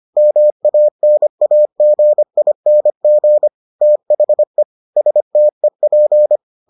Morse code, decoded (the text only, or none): MANAGING THE STEP